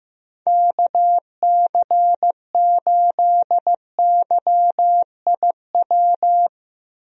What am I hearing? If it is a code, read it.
Morse code, heard as a keyed tone, about 15 wpm, 700 Hz. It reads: KC8YIW
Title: USA Call Signs - 6 Characters - Repeated 3x - 15wpm